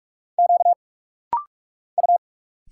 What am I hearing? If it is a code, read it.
Morse code, heard as a keyed tone, about 44 wpm, 700 Hz. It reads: Q U
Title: Single Letter-Number Alt - Rapid-Fire 45wpm